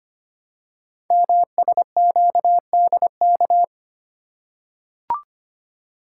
Morse code, heard as a keyed tone, about 25 wpm, 700 Hz. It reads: MSQDK
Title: Code Groups - 5 Letters - 25wpm